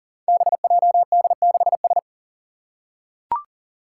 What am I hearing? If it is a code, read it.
Morse code, heard as a keyed tone, about 39 wpm, 690 Hz. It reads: BJD6S
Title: Five Challenging Letter-Number Combinations 40wpm